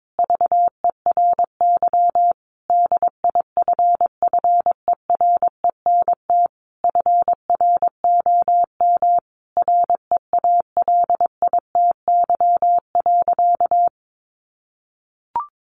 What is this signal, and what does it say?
Morse code, heard as a keyed tone, about 22 wpm, 700 Hz. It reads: VERY DIFFERENT FROM REALITY.
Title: Sentences from Top 1500 Words 22wpm